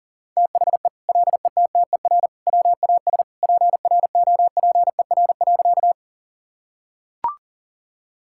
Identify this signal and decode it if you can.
Morse code, heard as a keyed tone, about 40 wpm, 700 Hz. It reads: THE LETTER WAS PROPER.
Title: Sentences from Top 1000 Words - Review Part 5 - 40wpm